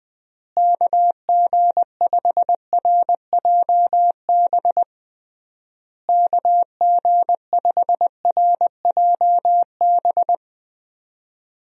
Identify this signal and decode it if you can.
Morse code, heard as a keyed tone, about 20 wpm, 700 Hz. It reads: KG5RJB KG5RJB
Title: USA Call Signs - 6 Characters - Repeated 3x - 20wpm